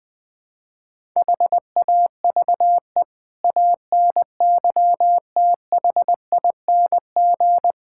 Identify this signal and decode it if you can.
Morse code, heard as a keyed tone, about 20 wpm, 700 Hz. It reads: HAVE ANYTHING
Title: Build A Phrase - 5-Word Phrases - 20wpm